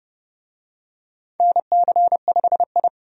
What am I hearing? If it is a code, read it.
Morse code, heard as a keyed tone, about 30 wpm, 700 Hz. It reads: NC5I
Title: USA Call Signs - 4 Characters - 30wpm